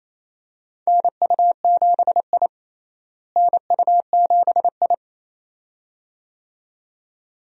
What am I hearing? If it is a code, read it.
Morse code, heard as a keyed tone, about 28 wpm, 700 Hz. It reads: NU7I NU7I
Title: USA Call Signs - 4 Characters - Repeated 2x - 28wpm